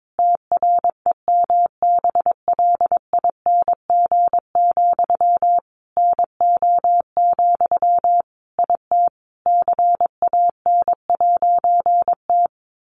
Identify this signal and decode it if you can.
Morse code, heard as a keyed tone, about 22 wpm, 700 Hz. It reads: TREMBLING, NO, IT CAN'T
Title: A Sound Of Thunder 22wpm